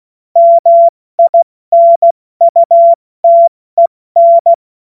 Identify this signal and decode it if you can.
Morse code, heard as a keyed tone, about 16 wpm, 680 Hz. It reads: MINUTEN
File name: Morsekurs / Witze / Witz 21. Dezember 2021, 70 F80